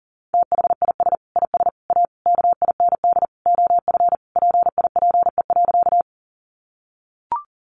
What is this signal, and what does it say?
Morse code, heard as a keyed tone, about 40 wpm, 700 Hz. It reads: THIS IS A KIND OF PIPE.
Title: Sentences from Top 1500 Words 40wpm